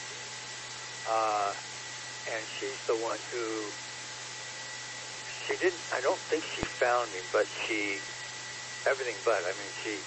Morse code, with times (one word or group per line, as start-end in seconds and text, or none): none